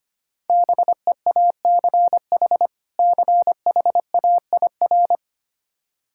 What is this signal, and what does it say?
Morse code, heard as a keyed tone, about 25 wpm, 700 Hz. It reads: BEACH CHAIR
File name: practice / 25wpm / Difficult Words - Speed-Racing with Hints - 25wpm